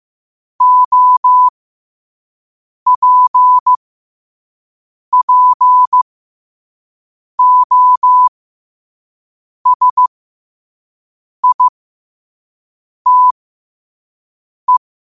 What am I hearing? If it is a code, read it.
Morse code, heard as a keyed tone, about 15 wpm, spaced as for 5 wpm, 1000 Hz. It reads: OPPOSITE
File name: 5wpm